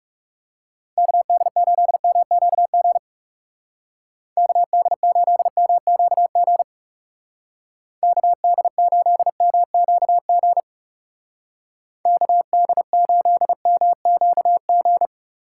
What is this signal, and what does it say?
Morse code, heard as a keyed tone, about 38 wpm, 700 Hz. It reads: KD8MQG KD8MQG KD8MQG KD8MQG